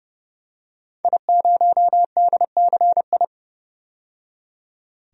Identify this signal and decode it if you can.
Morse code, heard as a keyed tone, about 30 wpm, 700 Hz. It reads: I0DCI
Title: DXCC-100 Call Signs - 5 Characters - 30wpm